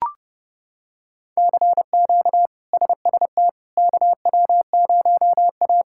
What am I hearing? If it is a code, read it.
Morse code, heard as a keyed tone, about 30 wpm, 700 Hz. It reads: CQ SST KW0A
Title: K1USN SST QSO Practice - 30wpm